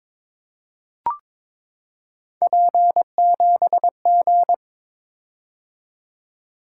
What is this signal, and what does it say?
Morse code, heard as a keyed tone, about 21 wpm, 700 Hz. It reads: P7G
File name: practice / 22wpm / Three Letter-Number Combinations - 22wpm